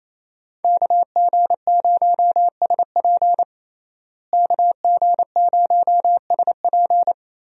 Morse code, heard as a keyed tone, about 28 wpm, 700 Hz. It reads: KG0SP KG0SP
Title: USA Call Signs - 5 Characters - Repeated 3x - 28wpm